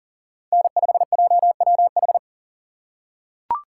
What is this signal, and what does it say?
Morse code, heard as a keyed tone, about 40 wpm, 700 Hz. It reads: N5JWH
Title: USA Call Signs - 5 Characters - 4x Character Spacing - 40wpm